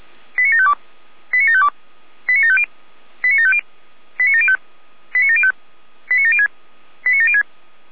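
Five-tone selective call (ZVEI), heard as quick runs of five tones, tones of about 70 ms. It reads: 89752 89752 89750 89750 89705 89705 89706 89706
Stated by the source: SDR receiver recording of ZVEI five-tone selective call 89715.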